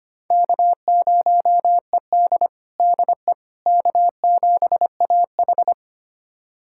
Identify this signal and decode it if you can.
Morse code, heard as a keyed tone, about 25 wpm, 700 Hz. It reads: K0ED DE K7AH
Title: USA Callsign DE Callsign - 4 Characters - 25wpm